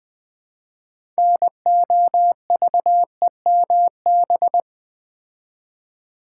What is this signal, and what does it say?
Morse code, heard as a keyed tone, about 20 wpm, 700 Hz. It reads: NOVEMB
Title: Build A Word - 2 Letters At A Time - 8-Letter Words - 20wpm